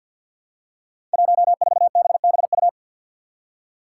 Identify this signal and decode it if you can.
Morse code, heard as a keyed tone, about 51 wpm, 700 Hz. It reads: 14BBU
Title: Five Challenging Letter-Number Combinations 50wpm